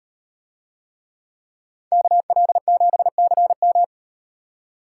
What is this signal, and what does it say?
Morse code, heard as a keyed tone, about 36 wpm, 700 Hz. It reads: KL7CM